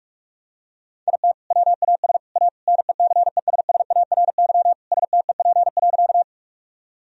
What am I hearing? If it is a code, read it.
Morse code, heard as a keyed tone, about 41 wpm, 700 Hz. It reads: IT WAS A NECESSARY STEP.
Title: Sentences from Top 1000 Words - Review Part 2 - 45wpm